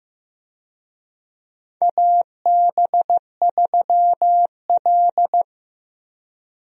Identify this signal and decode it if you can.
Morse code, heard as a keyed tone, about 15 wpm, 700 Hz. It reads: AB3L
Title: USA Call Signs - 4 Characters - Repeated 2x - 15wpm